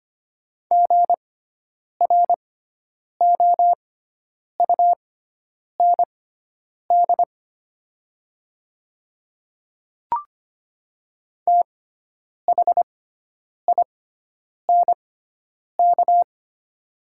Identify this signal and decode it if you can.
Morse code, heard as a keyed tone, about 25 wpm, spaced as for 9 wpm, 700 Hz. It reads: GROUND THINK
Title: 6x Character Spacing - Top 500 Words - 25wpm